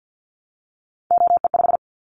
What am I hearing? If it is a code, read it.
Morse code, heard as a keyed tone, about 50 wpm, 700 Hz. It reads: OE5
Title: Callsign Prefixes for Top 50 DXCC - 50wpm